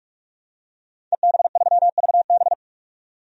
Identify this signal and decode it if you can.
Morse code, heard as a keyed tone, about 46 wpm, 700 Hz. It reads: EB3VB
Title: DXCC-100 Call Signs - 5 Characters - 45wpm